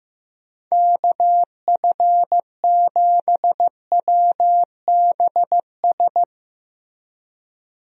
Morse code, heard as a keyed tone, about 16 wpm, 700 Hz. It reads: KF7WBS
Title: USA Call Signs - 6 Characters - Repeated 3x - 15wpm